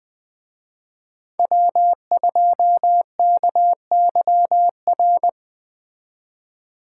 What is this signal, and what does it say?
Morse code, heard as a keyed tone, about 20 wpm, 700 Hz. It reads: W2KYR
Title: Speed Racing - USA 1x3 Call Signs 17wpm